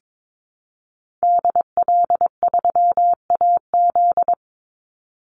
Morse code, heard as a keyed tone, about 22 wpm, 700 Hz. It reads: DL3AZ
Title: Call Signs - 22wpm